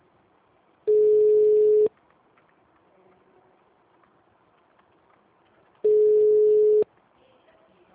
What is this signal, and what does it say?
Ringback tone (Europe, 425 Hz).